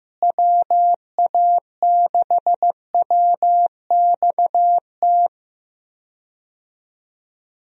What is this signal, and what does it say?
Morse code, heard as a keyed tone, about 15 wpm, 700 Hz. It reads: WA6WXT
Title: USA Call Signs - 6 Characters - Repeated 2x - 15wpm